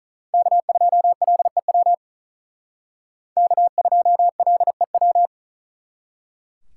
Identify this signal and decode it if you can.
Morse code, heard as a keyed tone, about 45 wpm, 700 Hz. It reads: K2LEW K2LEW